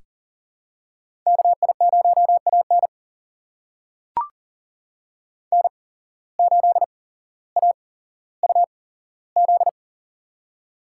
Morse code, heard as a keyed tone, about 40 wpm, spaced as for 17 wpm, 700 Hz. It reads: KI0AN N8AUZ
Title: USA Call Signs - 8x Character Spacing - 40wpm